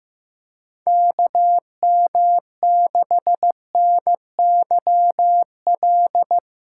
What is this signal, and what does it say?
Morse code, heard as a keyed tone, about 15 wpm, 700 Hz. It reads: KM6NYL
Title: USA Call Signs - 6 Characters - 2x Character Spacing - 15wpm